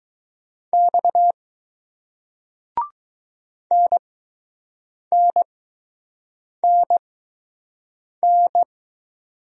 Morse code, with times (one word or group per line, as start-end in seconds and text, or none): 0.7-1.3 X
3.7-8.6 NNNN